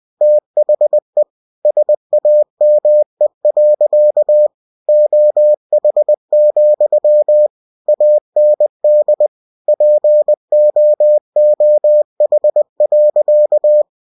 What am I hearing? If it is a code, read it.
Morse code, heard as a keyed tone, about 20 wpm, 600 Hz. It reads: THE SAME. OH, AND POOH.
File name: cw-books / winnie-the-pooh / winnie-20wpm-page-0107